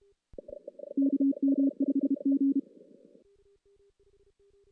Morse code, heard as a keyed tone, about 32 wpm, 280 Hz. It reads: KM5G
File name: 2016 JAN CW NAQP / 2016 Jan naqp k5ot - 0-5